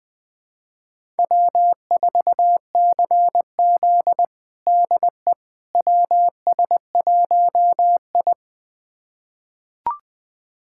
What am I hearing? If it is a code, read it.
Morse code, heard as a keyed tone, about 20 wpm, 700 Hz. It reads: W4CZ DE WS1I
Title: USA Callsign DE Callsign - 4 Characters - 20wpm